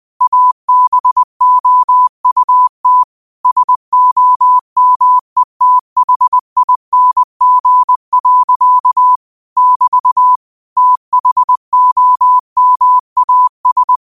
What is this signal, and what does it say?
Morse code, heard as a keyed tone, about 20 wpm, 1000 Hz. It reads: ABOUT SOMETHING. <BT> THOMAS